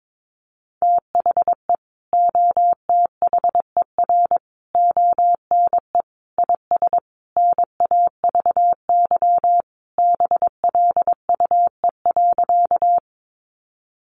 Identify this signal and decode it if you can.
Morse code, heard as a keyed tone, about 22 wpm, 700 Hz. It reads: THE OTHER ONE IS NAVY BLUE.
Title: Additional Easy Phrases 22wpm